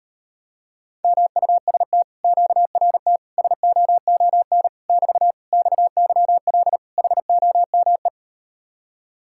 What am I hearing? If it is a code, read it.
Morse code, heard as a keyed tone, about 38 wpm, 700 Hz. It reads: MUST QRT SOON <BT> XYL HOME